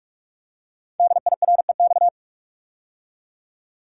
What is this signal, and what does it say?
Morse code, heard as a keyed tone, about 44 wpm, 700 Hz. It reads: DIREX
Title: Code Groups - 5 Letters - 45wpm